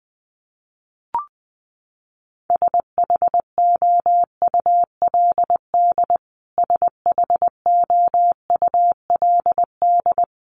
Morse code, heard as a keyed tone, about 20 wpm, 700 Hz. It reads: SHOULD SHOULD
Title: Top 300 Words Repeated 20wpm